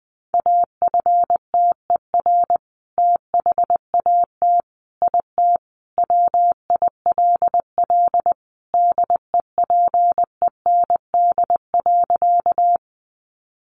Morse code, heard as a keyed tone, about 20 wpm, 700 Hz. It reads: AFTER THAT IT WILL DEPEND.
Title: Sentences from Top 1300 Words 20wpm